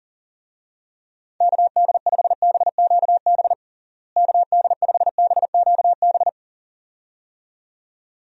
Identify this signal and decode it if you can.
Morse code, heard as a keyed tone, about 40 wpm, 700 Hz. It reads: KD5BQB KD5BQB
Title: USA Call Signs - 6 Characters - Repeated 2x - 40wpm